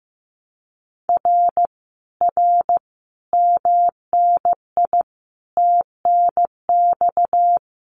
Morse code, heard as a keyed tone, about 15 wpm, 700 Hz. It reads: R R MNI TNX
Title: Common QSO Phrases Encore 15wpm